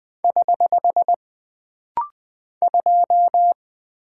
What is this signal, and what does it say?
Morse code, heard as a keyed tone, about 20 wpm, 700 Hz. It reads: <HH> 2
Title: Single Character (Letter:Number:Punctuation:Prosign) - Rapid-Fire - 20wpm